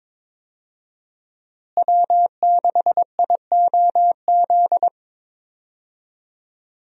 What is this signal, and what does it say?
Morse code, heard as a keyed tone, about 22 wpm, 700 Hz. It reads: W6IOZ
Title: USA Call Signs - 5 Characters - Repeated 2x - 22wpm